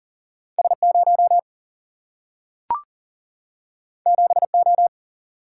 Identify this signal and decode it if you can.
Morse code, heard as a keyed tone, about 40 wpm, 700 Hz. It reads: S0 7O
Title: Single-Letter Single-Number 40wpm